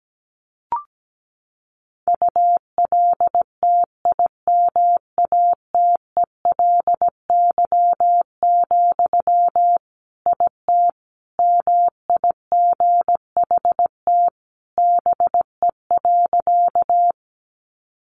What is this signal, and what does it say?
Morse code, heard as a keyed tone, about 17 wpm, 700 Hz. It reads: ULTIMATELY, IT MIGHT BE.
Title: Sentences from Top 1200 Words 17wpm